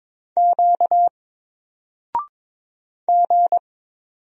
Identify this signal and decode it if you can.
Morse code, heard as a keyed tone, about 22 wpm, 700 Hz. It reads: Q G
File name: Single-Letter-Number-22wpm